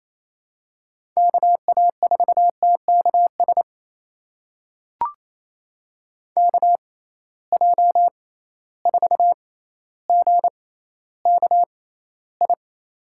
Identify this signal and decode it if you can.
Morse code, heard as a keyed tone, about 28 wpm, spaced as for 14 wpm, 700 Hz. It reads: KA4TKS KJ4GKI
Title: USA Call Signs - 6 Characters - 6x Character Spacing - 28wpm